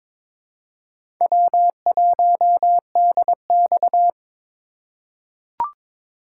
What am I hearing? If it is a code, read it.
Morse code, heard as a keyed tone, about 22 wpm, 700 Hz. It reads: W1DX
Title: K1USN SST QSO Practice - 22wpm